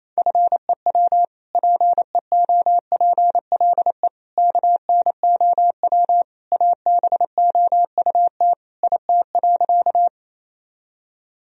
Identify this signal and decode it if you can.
Morse code, heard as a keyed tone, about 28 wpm, 700 Hz. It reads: FEW PEOPLE KNOW ABOUT IT.
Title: Sentences from Top 1000 Words - Review Part 1 - 28wpm